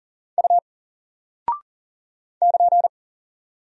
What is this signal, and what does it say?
Morse code, heard as a keyed tone, about 40 wpm, 700 Hz. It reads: U <KN>
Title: Single Character (Letter:Number:Punctuation:Prosign) 40wpm